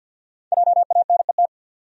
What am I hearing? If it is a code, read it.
Morse code, heard as a keyed tone, about 52 wpm, 700 Hz. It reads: JANET
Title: USA Common Names - Max 5 Letters - 50wpm